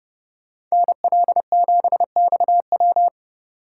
Morse code, heard as a keyed tone, about 33 wpm, 700 Hz. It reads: NL7XW